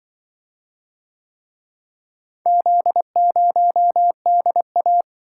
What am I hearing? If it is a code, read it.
Morse code, heard as a keyed tone, about 25 wpm, 700 Hz. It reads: Z0DA